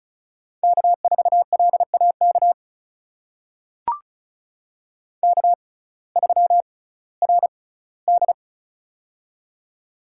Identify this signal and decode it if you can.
Morse code, heard as a keyed tone, about 36 wpm, spaced as for 19 wpm, 700 Hz. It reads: K4LAK K3RD
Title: USA Call Signs - 6x Character Spacing - 35wpm